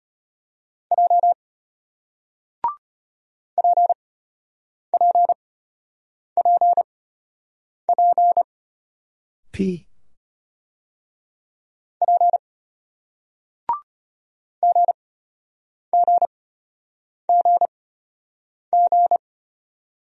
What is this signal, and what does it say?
Morse code, heard as a keyed tone, about 33 wpm, spaced as for 8 wpm, 700 Hz. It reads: J PPPP P GGGG